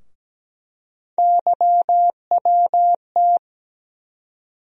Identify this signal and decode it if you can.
Morse code, heard as a keyed tone, about 17 wpm, 700 Hz. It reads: YWT
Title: USA Call Sign Postfixes - 3 Letters 17wpm